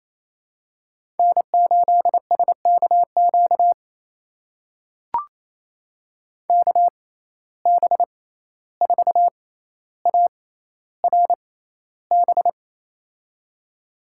Morse code, heard as a keyed tone, about 28 wpm, spaced as for 14 wpm, 700 Hz. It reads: N8SKQ KB4ARB
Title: USA Call Signs - 6x Character Spacing - 28wpm